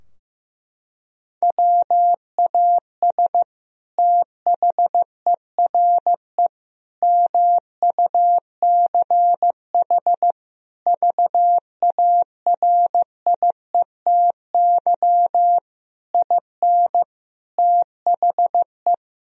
Morse code, heard as a keyed tone, about 16 wpm, 700 Hz. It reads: WAS THERE MUCH VARIETY IN THE
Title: Sentences from Top 1000 Words - Review Part 4 - 15wpm